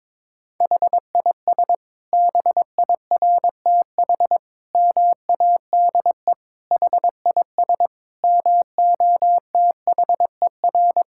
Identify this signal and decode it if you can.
Morse code, heard as a keyed tone, about 22 wpm, 700 Hz. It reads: HIS BIRTH MADE HIS MOTHER
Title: Sentences from Top 1000 Words - Review Part 5 - 22wpm